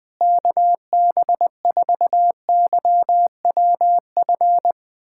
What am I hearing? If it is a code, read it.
Morse code, heard as a keyed tone, about 20 wpm, 700 Hz. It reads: KB4YWF
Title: USA Callsign DE Callsign - 6 Characters - 20wpm